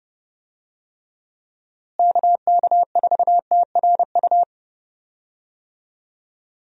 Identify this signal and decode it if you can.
Morse code, heard as a keyed tone, about 32 wpm, 700 Hz. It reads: KK4TRU
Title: USA Call Signs - 6 Characters - Repeated 3x - 30wpm